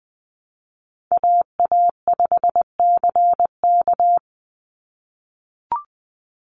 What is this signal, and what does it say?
Morse code, heard as a keyed tone, about 20 wpm, 700 Hz. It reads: AA5CK
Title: NA SOTA Chaser Callsigns 20wpm